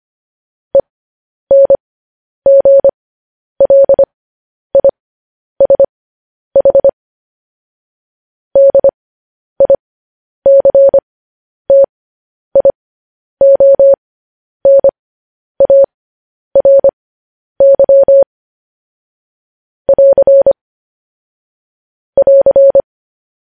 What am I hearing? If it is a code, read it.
Morse code, heard as a keyed tone, about 25 wpm, spaced as for 10 wpm, 550 Hz. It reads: ENGLISH DICTIONARY <AR> <AR>
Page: news-headlines-in-morse-code-at-10-wpm